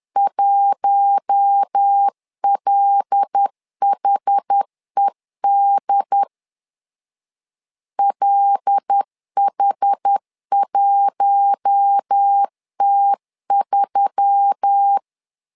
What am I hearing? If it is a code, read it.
Morse code, heard as a keyed tone, about 10 wpm, 800 Hz. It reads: 1LHED LH1T3
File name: LEZIONI / 5 LBVC / 5 L 10wpm 2259